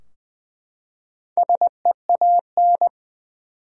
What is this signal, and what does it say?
Morse code, heard as a keyed tone, about 20 wpm, 700 Hz. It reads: SEAN
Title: USA Common Names - 20wpm